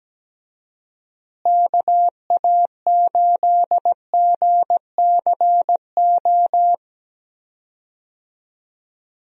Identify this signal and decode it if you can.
Morse code, heard as a keyed tone, about 17 wpm, 700 Hz. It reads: KA8GCO
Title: USA Call Signs - 6 Characters - Repeated 2x - 17wpm